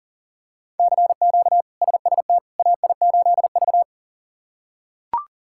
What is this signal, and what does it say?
Morse code, heard as a keyed tone, about 40 wpm, 700 Hz. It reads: CQ SST AI8V
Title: K1USN SST QSO Practice - 40wpm